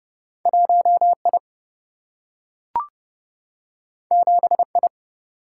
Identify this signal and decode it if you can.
Morse code, heard as a keyed tone, about 30 wpm, 700 Hz. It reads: 1I 7I
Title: Single-Letter Single-Number 30wpm